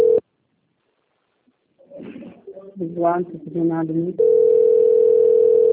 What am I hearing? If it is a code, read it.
Ringback tone (North America), 440 + 480 Hz, with speech.
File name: rg-1600-unknown- - - .927